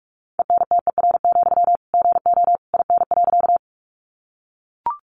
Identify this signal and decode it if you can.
Morse code, heard as a keyed tone, about 44 wpm, 700 Hz. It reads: ENTER, GO IN.
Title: Sentences from Top 1300 Words 45wpm